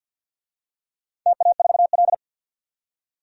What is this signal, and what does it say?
Morse code, heard as a keyed tone, about 50 wpm, 700 Hz. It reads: TA4L
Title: DXCC-100 Call Signs - 4 Characters - 50wpm